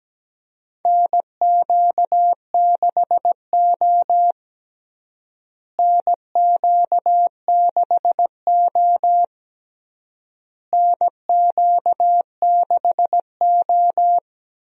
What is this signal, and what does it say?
Morse code, heard as a keyed tone, about 17 wpm, 700 Hz. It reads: NQ6O NQ6O NQ6O
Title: USA Call Signs - 4 Characters - Repeated 3x - 17wpm